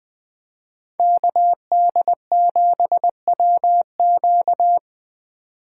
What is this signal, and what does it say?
Morse code, heard as a keyed tone, about 20 wpm, 700 Hz. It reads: KD7WQ